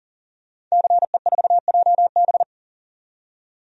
Morse code, heard as a keyed tone, about 40 wpm, 700 Hz. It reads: CE4JB
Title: DXCC-100 Call Signs - 5 Characters - 40wpm